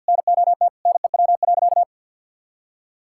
Morse code, heard as a keyed tone, about 50 wpm, 700 Hz. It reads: NOT NEW.
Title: Sentences from Top 1600 Words 50wpm